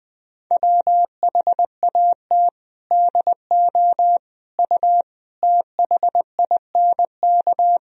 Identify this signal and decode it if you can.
Morse code, heard as a keyed tone, about 20 wpm, 700 Hz. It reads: WHAT DO U THINK
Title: Sentences from Top 200 Words and Abbreviations 20wpm